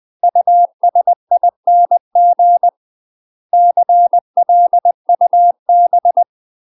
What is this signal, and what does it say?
Morse code, heard as a keyed tone, about 20 wpm, 700 Hz. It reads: USING CLUB
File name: UK MASTER QSOs TRAINING / 20WPM / UK MASTER QSOs 0012